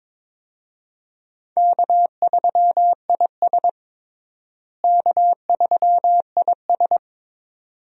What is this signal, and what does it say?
Morse code, heard as a keyed tone, about 22 wpm, 700 Hz. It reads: K3IS K3IS
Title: USA Call Signs - 4 Characters - Repeated 2x - 22wpm